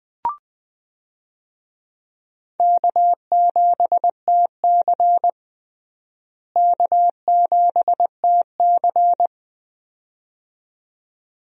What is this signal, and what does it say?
Morse code, heard as a keyed tone, about 21 wpm, 700 Hz. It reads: K7TC K7TC
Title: USA Call Signs - 4 Characters - Repeated 2x - 20wpm